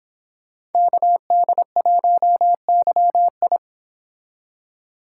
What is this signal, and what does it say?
Morse code, heard as a keyed tone, about 26 wpm, 700 Hz. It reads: KD1YI